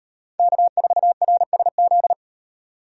Morse code, heard as a keyed tone, about 37 wpm, 700 Hz. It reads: K4RSZ